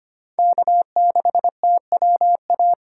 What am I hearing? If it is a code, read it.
Morse code, heard as a keyed tone, about 25 wpm, 700 Hz. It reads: K6TWA
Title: USA Call Signs - 2x Character Spacing - 25wpm